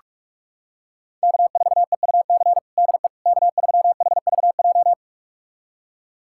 Morse code, heard as a keyed tone, about 46 wpm, 700 Hz. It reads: K4EUC DE K3HVJ
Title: USA Callsign DE Callsign - 5 Characters - 45wpm